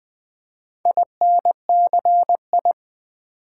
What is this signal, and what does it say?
Morse code, heard as a keyed tone, about 20 wpm, 700 Hz. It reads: INCI
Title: Build A Word - 4 Letters At A Time - 8-Letter Words - 20wpm